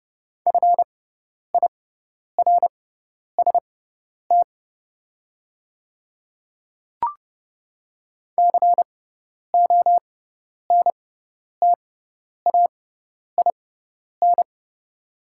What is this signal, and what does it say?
Morse code, heard as a keyed tone, about 31 wpm, spaced as for 10 wpm, 700 Hz. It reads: FIRST CONTAIN
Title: 6x Character Spacing - Top 500 Words - 30wpm